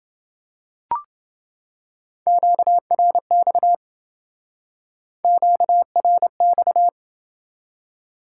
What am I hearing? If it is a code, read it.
Morse code, heard as a keyed tone, about 27 wpm, 700 Hz. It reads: QRX QRX